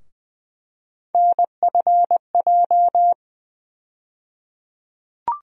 Morse code, heard as a keyed tone, about 20 wpm, 700 Hz. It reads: NFJ
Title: USA Call Sign Postfixes - 3 Letters 20wpm